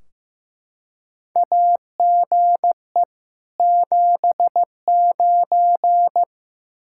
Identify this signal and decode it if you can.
Morse code, heard as a keyed tone, about 15 wpm, 700 Hz. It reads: AGE 79
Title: K1USN SST QSO Practice 15wpm